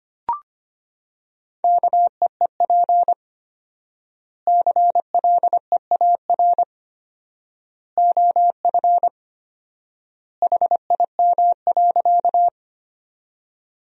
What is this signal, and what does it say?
Morse code, heard as a keyed tone, about 25 wpm, 700 Hz. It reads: KEEP CLEAR OF HIM.
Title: 4x Word-Spacing - Sentences from Top 500 Words - 25wpm